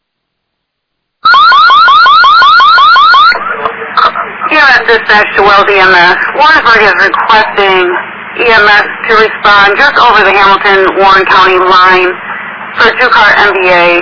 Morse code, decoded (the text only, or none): none